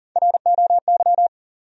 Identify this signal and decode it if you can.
Morse code, heard as a keyed tone, about 40 wpm, 700 Hz. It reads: ROY
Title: USA Common Names - Max 3 Letters - 40wpm